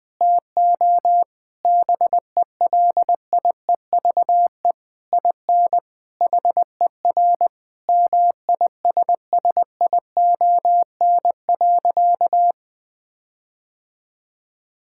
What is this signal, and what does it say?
Morse code, heard as a keyed tone, about 21 wpm, 700 Hz. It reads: TO BELIEVE IN HER MISSION.
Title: Sentences from Top 1600 Words 20wpm